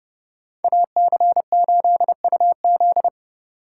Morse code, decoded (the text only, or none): AC8UZ